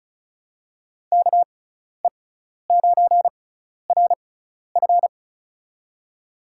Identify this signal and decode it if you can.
Morse code, heard as a keyed tone, about 34 wpm, spaced as for 12 wpm, 700 Hz. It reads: KE9RF
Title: USA Call Signs - 5 Characters - 6x Character Spacing - 35wpm